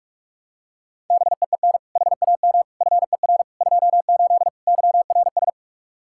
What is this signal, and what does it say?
Morse code, heard as a keyed tone, about 42 wpm, 700 Hz. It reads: BEEN HAM FER 28 YRS
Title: K1USN SST QSO Practice 45wpm